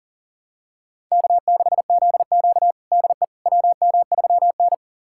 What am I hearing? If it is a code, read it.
Morse code, heard as a keyed tone, about 37 wpm, 700 Hz. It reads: K6ZQ DE WM3N
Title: USA Callsign DE Callsign - 4 Characters - 40wpm